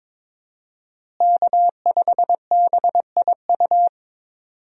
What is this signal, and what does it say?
Morse code, heard as a keyed tone, about 22 wpm, 700 Hz. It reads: K5BIU
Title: USA Call Signs - 5 Characters - Repeated 2x - 22wpm